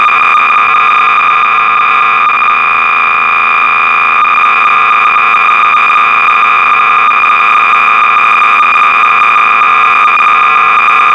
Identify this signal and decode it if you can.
Cassette data (Kansas City type).